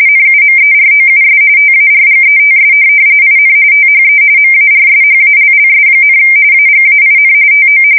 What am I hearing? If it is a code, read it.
RTTY, about 45.45 baud. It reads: NUMEROUS TARGETS IN THE MARSHALLS, AGAIN TO
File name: fsk170 / USS-Massachusetts